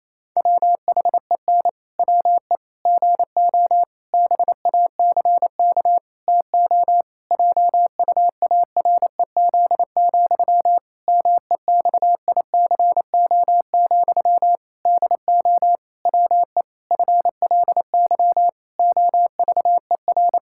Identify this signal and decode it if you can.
Morse code, heard as a keyed tone, about 27 wpm, 700 Hz. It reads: WHEN WE GO BACK TO JUAREZ, MEXICO, DO WE FLY OVER
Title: Pangram Challenge v2 - 28wpm